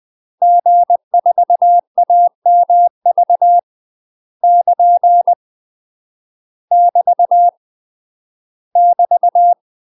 Morse code, heard as a keyed tone, about 20 wpm, 700 Hz. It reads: G4AMV <KN> <BT> <BT>
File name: UK MASTER QSOs TRAINING / 20WPM / UK MASTER QSOs 0023